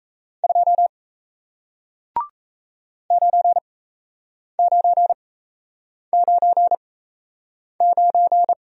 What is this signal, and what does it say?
Morse code, heard as a keyed tone, about 37 wpm, spaced as for 8 wpm, 700 Hz. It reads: 2 9999